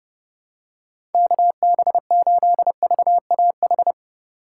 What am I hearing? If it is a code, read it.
Morse code, heard as a keyed tone, about 30 wpm, 700 Hz. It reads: KB8VAH
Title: USA Call Signs - 3x Character Spacing - 30wpm